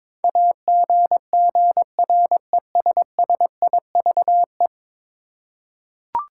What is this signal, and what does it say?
Morse code, heard as a keyed tone, about 22 wpm, 700 Hz. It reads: AGGRESSIVE
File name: practice / 22wpm / 10 Letter Words - First 3 Letters Given - 22wpm